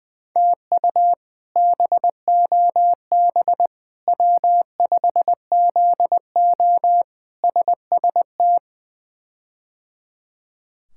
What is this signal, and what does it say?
Morse code, heard as a keyed tone, about 20 wpm, 700 Hz. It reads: TU BOB W5ZO SST
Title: K1USN SST QSO Practice - 20wpm